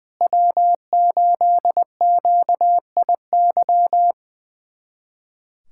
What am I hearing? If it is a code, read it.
Morse code, heard as a keyed tone, about 21 wpm, 700 Hz. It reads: W8QIY
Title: USA Callsign DE Callsign - 5 Characters - 20wpm